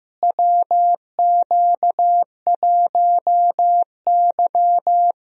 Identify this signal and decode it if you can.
Morse code, heard as a keyed tone, about 15 wpm, 700 Hz. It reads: WQ1Y
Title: USA Call Signs - 4 Characters - Repeated 2x - 15wpm